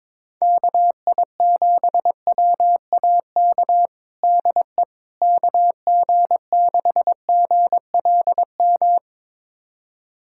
Morse code, heard as a keyed tone, about 22 wpm, 700 Hz. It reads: KI7WAK DE KG6GLM
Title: USA Callsign DE Callsign - 6 Characters - 22wpm